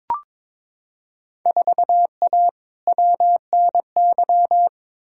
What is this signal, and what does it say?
Morse code, heard as a keyed tone, about 22 wpm, 700 Hz. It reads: 4A WNY
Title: Field Day Exchanges Encore - 22wpm